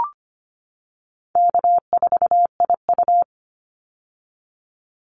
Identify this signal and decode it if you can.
Morse code, heard as a keyed tone, about 25 wpm, 700 Hz. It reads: K4IU
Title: Easy Call Signs - 25wpm